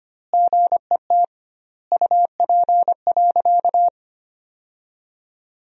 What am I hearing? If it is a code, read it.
Morse code, heard as a keyed tone, about 25 wpm, 700 Hz. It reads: GET UP.
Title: 2x Word-Spacing - Sentences from Top 500 Words - 25wpm